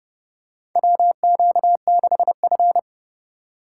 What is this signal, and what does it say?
Morse code, heard as a keyed tone, about 32 wpm, 700 Hz. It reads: WQ6F
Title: USA Call Signs - 2x Character Spacing - 30wpm